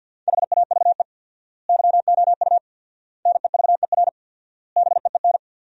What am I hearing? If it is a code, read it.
Morse code, heard as a keyed tone, about 47 wpm, 700 Hz. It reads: HAVE YOU NEVER BEEN
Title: 4x Word-Spacing - Sentences from Top 300 Words - 50wpm